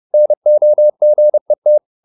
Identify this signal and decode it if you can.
Morse code, heard as a keyed tone, about 30 wpm, 600 Hz. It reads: NOGET
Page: Ferie 30 wpm